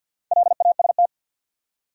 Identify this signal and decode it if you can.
Morse code, heard as a keyed tone, about 63 wpm, 700 Hz. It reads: LAST